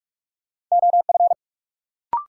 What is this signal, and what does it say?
Morse code, heard as a keyed tone, about 46 wpm, 700 Hz. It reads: OF
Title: Sentences from Top 100 Words - New Word Drill - 45wpm